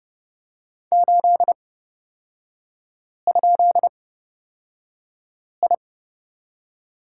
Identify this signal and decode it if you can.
Morse code, heard as a keyed tone, about 29 wpm, spaced as for 5 wpm, 690 Hz. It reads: 8?I
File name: course / 30:10wpm / 236 Lesson 38e - Letter q - Three Characters 30wpm